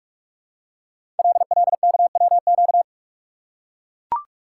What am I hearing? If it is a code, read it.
Morse code, heard as a keyed tone, about 44 wpm, 700 Hz. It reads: LLKWQ
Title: Code Groups - 5 Letter:Numbers - 45wpm